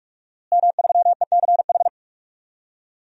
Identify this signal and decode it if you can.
Morse code, heard as a keyed tone, about 41 wpm, 700 Hz. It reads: M3ECH
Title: DXCC-100 Call Signs - 5 Characters - Repeated 2x - 45wpm